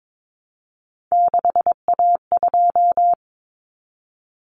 Morse code, heard as a keyed tone, about 22 wpm, 700 Hz. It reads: 6A2
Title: Common Cut Numbers - 3-Digit 22wpm